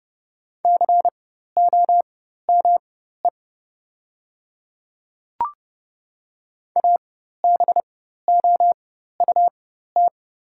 Morse code, heard as a keyed tone, about 30 wpm, spaced as for 14 wpm, 700 Hz. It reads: COME ABOUT
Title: 4x Character Spacing - Top 100 Words - 30wpm